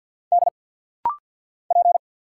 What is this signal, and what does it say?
Morse code, heard as a keyed tone, about 50 wpm, 690 Hz. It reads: D P
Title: Single Letter-Number Alt - Rapid-Fire 50wpm